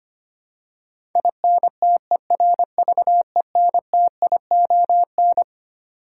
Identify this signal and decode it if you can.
Morse code, heard as a keyed tone, about 25 wpm, 700 Hz. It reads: INTERVENTION
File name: Twelve Letter Words - 25wpm